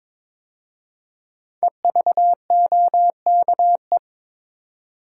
Morse code, heard as a keyed tone, about 22 wpm, 690 Hz. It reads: EVOKE
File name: 5 Letter Words - First 3 Letters Given - 22wpm